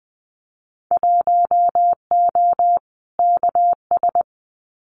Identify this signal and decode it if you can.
Morse code, heard as a keyed tone, about 20 wpm, 700 Hz. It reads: 1O KS
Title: Winter Field Day Exchanges 20wpm